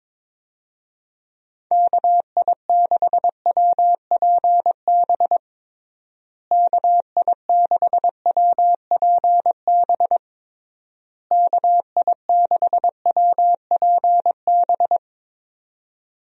Morse code, heard as a keyed tone, about 21 wpm, 700 Hz. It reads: KI6WPB KI6WPB KI6WPB
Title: USA Call Signs - 6 Characters - Repeated 3x - 22wpm